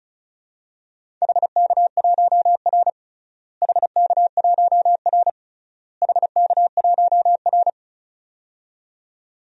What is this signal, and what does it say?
Morse code, heard as a keyed tone, about 36 wpm, 700 Hz. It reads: HK1R HK1R HK1R
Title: DXCC-100 Call Signs - 4 Characters - Repeated 3x - 35wpm